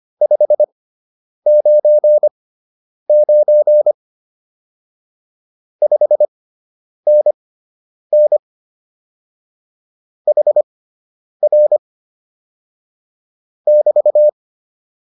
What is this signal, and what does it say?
Morse code, heard as a keyed tone, about 25 wpm, spaced as for 9 wpm, 600 Hz. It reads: 599 5NN HR <BT>